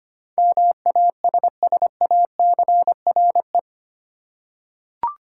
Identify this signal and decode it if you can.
Morse code, heard as a keyed tone, about 25 wpm, 700 Hz. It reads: MASSACRE
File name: Eight Letter Words - 25wpm